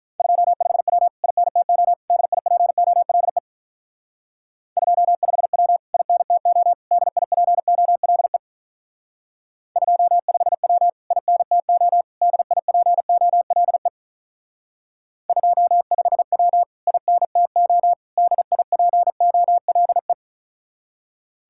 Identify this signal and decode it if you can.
Morse code, heard as a keyed tone, about 43 wpm, 700 Hz. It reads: 25W INTO DIPOLE 25W INTO DIPOLE 25W INTO DIPOLE 25W INTO DIPOLE
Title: Speed-Racing Common QSO Phrases Encore - 35wpm